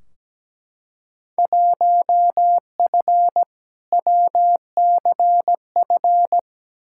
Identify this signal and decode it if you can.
Morse code, heard as a keyed tone, about 17 wpm, 700 Hz. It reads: 1F WCF
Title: Field Day Exchanges - 17wpm